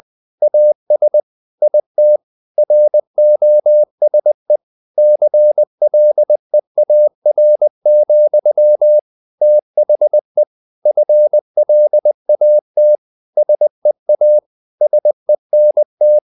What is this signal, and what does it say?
Morse code, heard as a keyed tone, about 20 wpm, 600 Hz. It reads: AS IT ROSE CLEAR, THE FLAT SEA SENT
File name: the old man and the sea / the old man and the sea part 1 20wpm